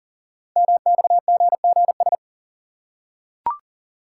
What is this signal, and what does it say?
Morse code, heard as a keyed tone, about 42 wpm, 700 Hz. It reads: MXGGS
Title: Code Groups - 5 Letters - 40wpm